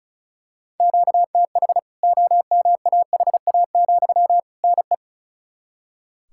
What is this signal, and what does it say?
Morse code, heard as a keyed tone, about 34 wpm, 700 Hz. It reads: QTH OMAHA, NE
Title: K1USN SST QSO Practice 35wpm